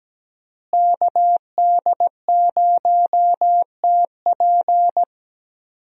Morse code, heard as a keyed tone, about 17 wpm, 700 Hz. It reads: KD0TP
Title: USA Call Signs - 5 Characters - Repeated 3x - 17wpm